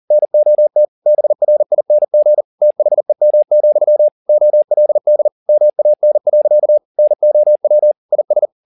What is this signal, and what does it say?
Morse code, heard as a keyed tone, about 40 wpm, 600 Hz. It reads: NOT BRING THEM, OLD MAN. NOW IS
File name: the old man and the sea 2 40wpm